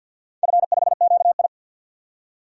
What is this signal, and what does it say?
Morse code, heard as a keyed tone, about 52 wpm, 700 Hz. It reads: F5QI
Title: DXCC-100 Call Signs - 4 Characters - 50wpm